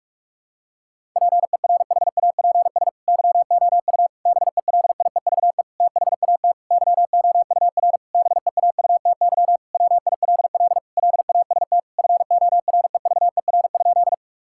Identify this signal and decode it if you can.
Morse code, heard as a keyed tone, about 46 wpm, 700 Hz. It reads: PERHAPS YOU BELIEVE THAT YOUR BEAUTY WILL LAST FOREVER?
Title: Sentences from Top 1600 Words 45wpm